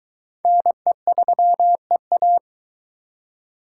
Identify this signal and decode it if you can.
Morse code, heard as a keyed tone, about 21 wpm, 700 Hz. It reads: NE3EA